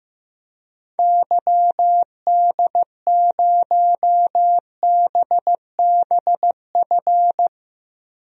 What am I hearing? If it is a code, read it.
Morse code, heard as a keyed tone, about 15 wpm, 690 Hz. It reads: YD0BBF
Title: DXCC-100 Call Signs - 6 Characters - 15wpm